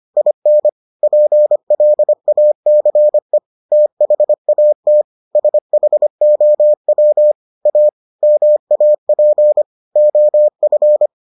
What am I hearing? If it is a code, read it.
Morse code, heard as a keyed tone, about 25 wpm, 600 Hz. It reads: IN PLACE THAT SHOW A MAP OF